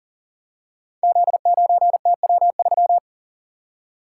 Morse code, heard as a keyed tone, about 42 wpm, 700 Hz. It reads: Z9TW3
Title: Code Groups - 5 Letter:Numbers - 40wpm